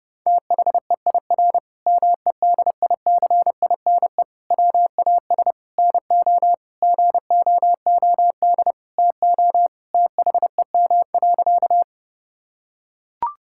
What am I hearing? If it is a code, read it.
Morse code, heard as a keyed tone, about 30 wpm, 700 Hz. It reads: THEIR MEDICINE WAS NO GOOD TO THEM.
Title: Sentences from Top 1000 Words - Review Part 6 - 30wpm